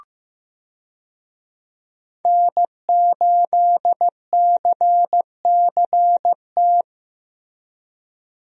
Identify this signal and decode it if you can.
Morse code, heard as a keyed tone, about 15 wpm, 700 Hz. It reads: N8CCT
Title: USA Call Signs - 5 Characters - Repeated 3x - 15wpm